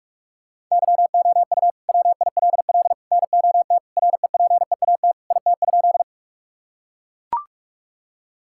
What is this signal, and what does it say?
Morse code, heard as a keyed tone, about 41 wpm, 700 Hz. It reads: YOU WILL NOT REPEAT IT?
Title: Sentences from Top 1600 Words 45wpm